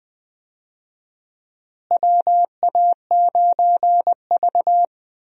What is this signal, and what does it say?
Morse code, heard as a keyed tone, about 20 wpm, 700 Hz. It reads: WA9V
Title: USA Call Signs - 4 Characters - Repeated 2x - 20wpm